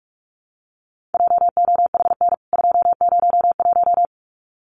Morse code, heard as a keyed tone, about 44 wpm, 700 Hz. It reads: JOHN 201
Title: CWT Exchange 45wpm